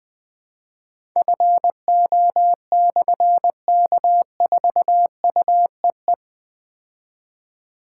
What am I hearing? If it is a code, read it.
Morse code, heard as a keyed tone, about 20 wpm, 700 Hz. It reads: FO/K4UEE
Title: Difficult Call Signs 20wpm